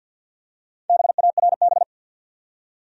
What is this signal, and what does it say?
Morse code, heard as a keyed tone, about 49 wpm, 700 Hz. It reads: BARB
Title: USA Common Names - Max 4 Letters - 50wpm